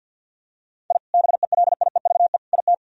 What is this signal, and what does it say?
Morse code, heard as a keyed tone, about 52 wpm, 700 Hz. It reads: I BELIEVE IT
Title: Sentences from Top 1000 Words - Review Part 1 - 50wpm